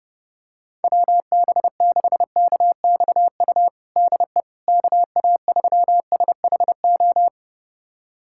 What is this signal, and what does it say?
Morse code, heard as a keyed tone, about 30 wpm, 700 Hz. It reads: WB6KXU DE KA3SHO
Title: USA Callsign DE Callsign 30wpm